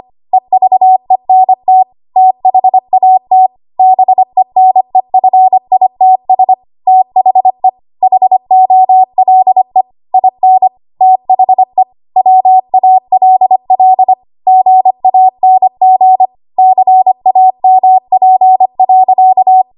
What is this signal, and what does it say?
Morse code, heard as a keyed tone, about 25 wpm, 750 Hz. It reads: EVENT THAT BENEFITS THE HOLE IN THE WALL GANG CAMP.
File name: Morse / 25WPM